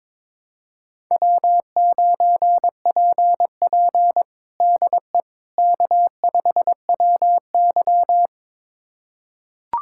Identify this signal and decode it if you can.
Morse code, heard as a keyed tone, about 22 wpm, 700 Hz. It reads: W9PP DE K5WY
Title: USA Callsign DE Callsign - 4 Characters - 22wpm